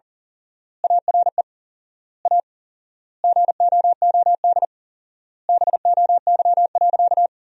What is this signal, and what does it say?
Morse code, heard as a keyed tone, about 40 wpm, 700 Hz. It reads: ARE A GOOD BOY.
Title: 4x Word-Spacing - Sentences from Top 200 Words - 40wpm